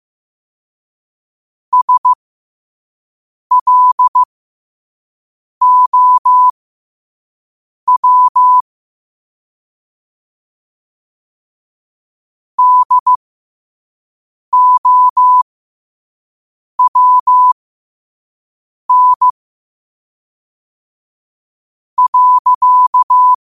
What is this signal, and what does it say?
Morse code, heard as a keyed tone, about 15 wpm, spaced as for 5 wpm, 1000 Hz. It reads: SLOW DOWN.
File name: qotdfiles / 5wpm